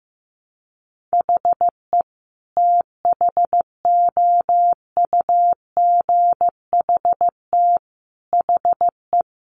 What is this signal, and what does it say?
Morse code, heard as a keyed tone, about 15 wpm, 700 Hz. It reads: HE THOUGHT HE
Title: Sentences from Top 600 Words 15wpm